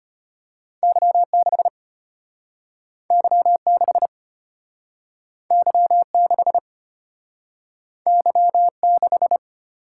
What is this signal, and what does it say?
Morse code, heard as a keyed tone, about 33 wpm, 700 Hz. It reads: Y6 Y6 Y6 Y6